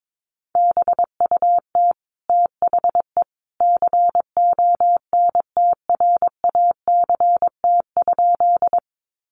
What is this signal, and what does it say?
Morse code, heard as a keyed tone, about 22 wpm, 700 Hz. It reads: BUT THE CONTRACT?
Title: Sentences from Top 1400 Words 22wpm